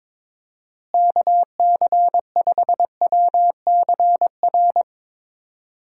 Morse code, heard as a keyed tone, about 22 wpm, 700 Hz. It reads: KC5WCR
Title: USA Call Signs - 6 Characters - Repeated 2x - 22wpm